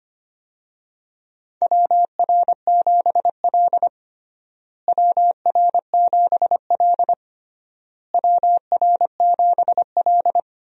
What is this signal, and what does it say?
Morse code, heard as a keyed tone, about 25 wpm, 700 Hz. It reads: WR7L WR7L WR7L
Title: USA Call Signs - 4 Characters - Repeated 3x - 25wpm